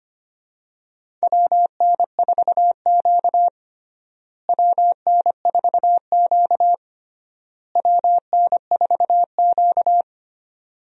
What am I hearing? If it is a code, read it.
Morse code, heard as a keyed tone, about 25 wpm, 700 Hz. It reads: WN4Q WN4Q WN4Q
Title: USA Call Signs - 4 Characters - Repeated 3x - 25wpm